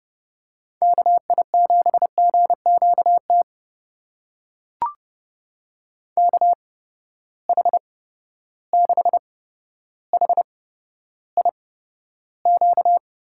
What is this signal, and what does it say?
Morse code, heard as a keyed tone, about 30 wpm, spaced as for 13 wpm, 700 Hz. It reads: KI7GQT KH6HIQ
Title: USA Call Signs - 6 Characters - 8x Character Spacing - 30wpm